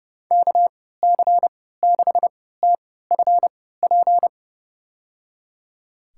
Morse code, heard as a keyed tone, about 30 wpm, spaced as for 17 wpm, 690 Hz. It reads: KC6TFP
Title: USA Call Signs - 3x Character Spacing - 30wpm